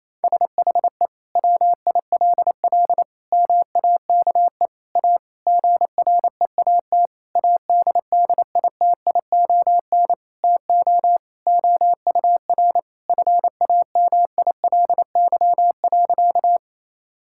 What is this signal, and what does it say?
Morse code, heard as a keyed tone, about 28 wpm, 700 Hz. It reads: SHE WILL MAKE A GREAT ADDITION TO OUR FAMILY.
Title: Sentences from Top 1000 Words - Review Part 6 - 28wpm